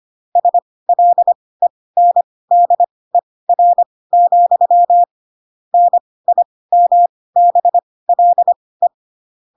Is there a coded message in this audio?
Yes: Morse code from 0.4 to 8.9 s, heard as a keyed tone, about 25 wpm, spaced as for 18 wpm, 700 Hz. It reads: SLENDER, NIMBLE